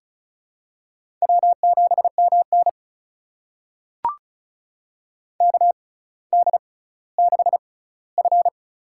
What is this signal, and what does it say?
Morse code, heard as a keyed tone, about 36 wpm, spaced as for 18 wpm, 700 Hz. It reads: W7MN KD6F
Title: USA Call Signs - 4 Characters - 6x Character Spacing - 35wpm